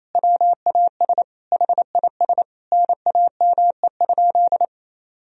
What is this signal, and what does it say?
Morse code, heard as a keyed tone, about 28 wpm, 700 Hz. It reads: WAS HIS NAME?
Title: Sentences from Top 200 Words and Abbreviations 28wpm